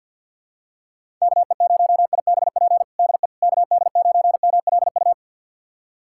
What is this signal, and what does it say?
Morse code, heard as a keyed tone, about 46 wpm, 700 Hz. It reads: KE0IBP DE KD9MLU
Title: USA Callsign DE Callsign - 6 Characters - 50wpm